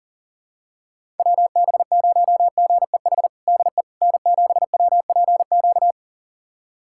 Morse code, heard as a keyed tone, about 40 wpm, 700 Hz. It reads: WB0GEH DE N7WPQ
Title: USA Callsign DE Callsign 40wpm